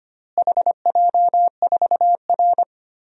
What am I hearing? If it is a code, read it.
Morse code, heard as a keyed tone, about 25 wpm, 700 Hz. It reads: HJ4R
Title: DXCC-100 Call Signs - 4 Characters - Repeated 2x - 25wpm